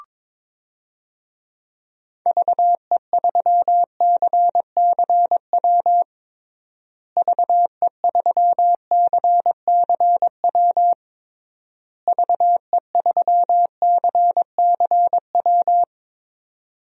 Morse code, heard as a keyed tone, about 22 wpm, 700 Hz. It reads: VE3CCW VE3CCW VE3CCW
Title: DXCC-100 Call Signs - 6 Characters - Repeated 3x - 22wpm